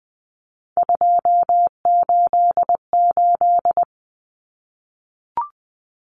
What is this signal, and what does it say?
Morse code, heard as a keyed tone, about 20 wpm, 700 Hz. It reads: 288
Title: Common Cut Numbers - 3-Digit 20wpm